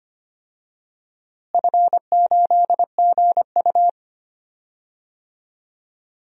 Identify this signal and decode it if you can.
Morse code, heard as a keyed tone, about 25 wpm, 700 Hz. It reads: F8GU
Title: DXCC-100 Call Signs - 4 Characters - Repeated 2x - 25wpm